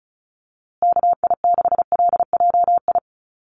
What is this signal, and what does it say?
Morse code, heard as a keyed tone, about 34 wpm, 700 Hz. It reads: KI6LJI
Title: USA Call Signs Repeated 2x 35wpm